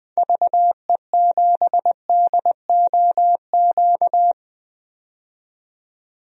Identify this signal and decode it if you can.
Morse code, heard as a keyed tone, about 20 wpm, 700 Hz. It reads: VE7DOQ
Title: DXCC-100 Call Signs - 6 Characters - 20wpm